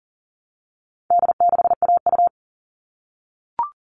Morse code, heard as a keyed tone, about 40 wpm, 700 Hz. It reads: D6AU
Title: All Cut Numbers - 4-Digit 40wpm